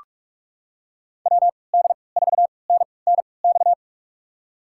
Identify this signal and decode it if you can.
Morse code, heard as a keyed tone, about 46 wpm, spaced as for 25 wpm, 700 Hz. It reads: WD4NNX
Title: USA Call Signs - 3x Character Spacing - 45wpm